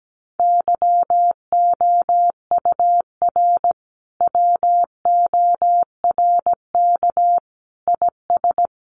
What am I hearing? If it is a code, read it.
Morse code, heard as a keyed tone, about 17 wpm, 700 Hz. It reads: YOUR WORK IS
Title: Sentences from Top 1100 Words 17wpm